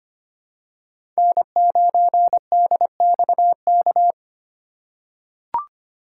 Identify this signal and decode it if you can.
Morse code, heard as a keyed tone, about 25 wpm, 700 Hz. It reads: N9DXK
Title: USA Call Signs - 5 Characters - 4x Character Spacing - 25wpm